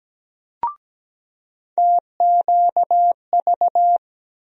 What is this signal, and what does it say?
Morse code, heard as a keyed tone, about 17 wpm, 700 Hz. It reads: TQV
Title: USA Call Sign Postfixes - 3 Letters 17wpm